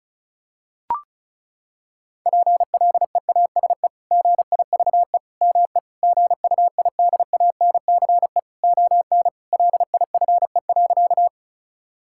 Morse code, heard as a keyed tone, about 36 wpm, 700 Hz. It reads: PLEASE GIVE ME GUIDANCE ON LIFE.
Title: Sentences from Top 1600 Words 35wpm